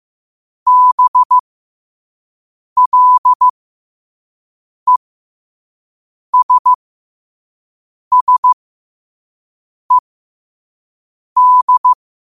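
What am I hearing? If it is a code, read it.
Morse code, heard as a keyed tone, about 15 wpm, spaced as for 5 wpm, 1000 Hz. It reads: BLESSED
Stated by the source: Quote-of-the-day Morse code practice at 5 wpm.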